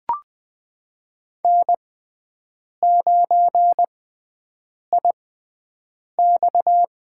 Morse code, heard as a keyed tone, about 20 wpm, spaced as for 7 wpm, 700 Hz. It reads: N9IX
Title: USA Call Signs - 4 Characters - 6x Character Spacing - 20wpm